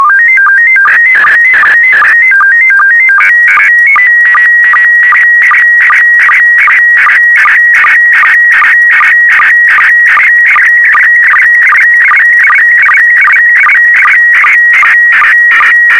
SSTV (slow-scan television), PD 50.
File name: sstv / PD50-TEST